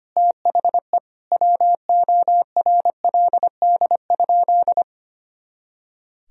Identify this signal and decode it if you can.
Morse code, heard as a keyed tone, about 25 wpm, 700 Hz. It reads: THE WORLD?
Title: Sentences from Top 200 Words and Abbreviations 25wpm